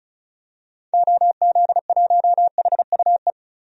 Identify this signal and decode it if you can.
Morse code, heard as a keyed tone, about 36 wpm, 700 Hz. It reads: OZ1HUE
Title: DXCC-100 Call Signs - 6 Characters - 35wpm